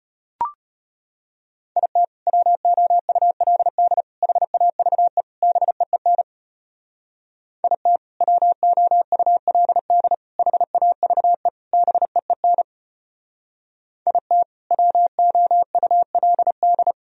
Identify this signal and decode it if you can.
Morse code, heard as a keyed tone, about 34 wpm, 700 Hz. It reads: IT WOULD HAVE BEEN IT WOULD HAVE BEEN IT WOULD